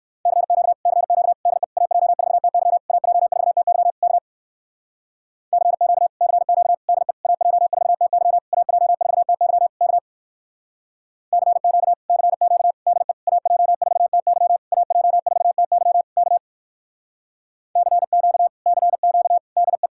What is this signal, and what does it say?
Morse code, heard as a keyed tone, about 58 wpm, 700 Hz. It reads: CQ CQ DE AJ4TY AJ4TY K CQ CQ DE AJ4TY AJ4TY K CQ CQ DE AJ4TY AJ4TY K CQ CQ DE